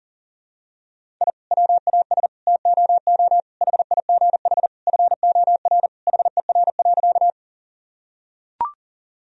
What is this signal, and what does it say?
Morse code, heard as a keyed tone, about 40 wpm, 700 Hz. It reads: I WAS TOO HIGH FOR HER.
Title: Sentences from Top 1000 Words - Review Part 1 - 40wpm